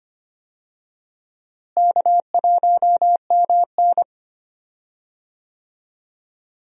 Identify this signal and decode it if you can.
Morse code, heard as a keyed tone, about 25 wpm, 700 Hz. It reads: K1MN
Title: USA Call Signs - 4 Characters - Repeated 3x - 25wpm